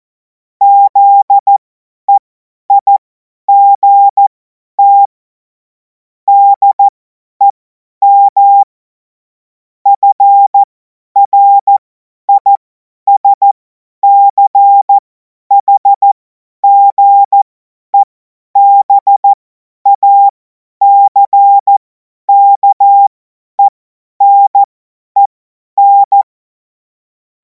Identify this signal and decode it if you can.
Morse code, heard as a keyed tone, about 14 wpm, spaced as for 10 wpm, 800 Hz. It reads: ZEIGT DEM FRISCHGEBACKENEN
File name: Morsekurs / Witze / Witz 09. Juni 2026 50 F70